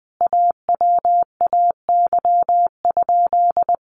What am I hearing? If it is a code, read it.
Morse code, heard as a keyed tone, about 20 wpm, 700 Hz. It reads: AWAY?
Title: Sentences from Top 1200 Words 20wpm